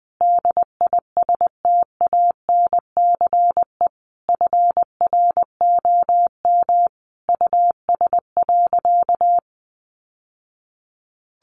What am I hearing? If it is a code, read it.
Morse code, heard as a keyed tone, about 20 wpm, 700 Hz. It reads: DISTANCE FROM US.
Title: Sentences from Top 1200 Words 20wpm